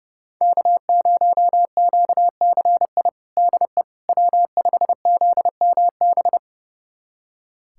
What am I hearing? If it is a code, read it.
Morse code, heard as a keyed tone, about 32 wpm, 700 Hz. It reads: K0QCI DE W5ZMB
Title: USA Callsign DE Callsign - 5 Characters - 30wpm